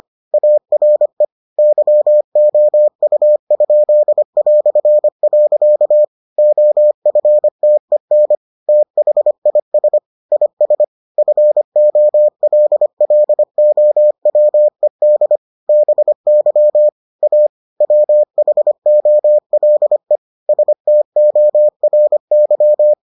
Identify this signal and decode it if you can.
Morse code, heard as a keyed tone, about 25 wpm, 600 Hz. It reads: ARE YOU?". OFTEN THIS IS FOLLOWED BY A WHOLE STORY